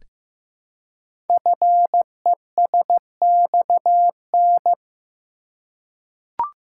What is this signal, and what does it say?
Morse code, heard as a keyed tone, about 15 wpm, 700 Hz. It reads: FESXN
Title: Code Groups - 5 Letters - 15wpm